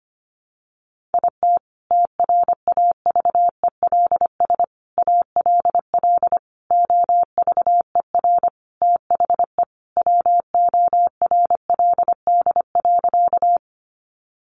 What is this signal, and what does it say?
Morse code, heard as a keyed tone, about 25 wpm, 700 Hz. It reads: IT TRAVELS ALL OVER THE WORLD.
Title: Additional Easy Phrases 25wpm